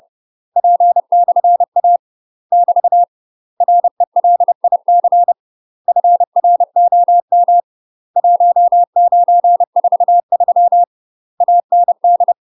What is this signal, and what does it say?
Morse code, heard as a keyed tone, about 30 wpm, 700 Hz. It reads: P/A <BT> RELIC FROM 1943 AND
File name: UK MASTER QSOs TRAINING / 30WPM / UK MASTER QSOs 0022